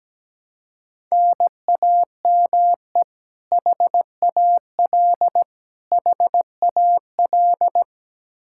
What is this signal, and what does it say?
Morse code, heard as a keyed tone, about 17 wpm, 700 Hz. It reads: NAME HAL HAL
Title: Practice QSO - Exchange 1 - Beginning - 17wpm